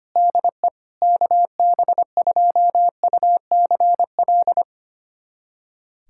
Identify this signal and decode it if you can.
Morse code, heard as a keyed tone, about 25 wpm, 700 Hz. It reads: DE KB2UCL
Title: USA Callsign DE Callsign 25wpm